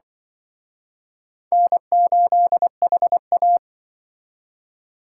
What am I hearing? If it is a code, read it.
Morse code, heard as a keyed tone, about 23 wpm, 700 Hz. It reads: N8HA
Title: Speed Racing - USA 1x2 Call Signs 20wpm